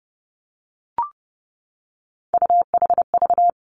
Morse code, heard as a keyed tone, about 30 wpm, 700 Hz. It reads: UHV
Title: Three Challenging Letter-Number Combinations 30wpm